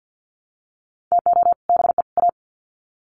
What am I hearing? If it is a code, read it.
Morse code, heard as a keyed tone, about 50 wpm, 700 Hz. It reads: TO BE A
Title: Sets of 3 Words 50wpm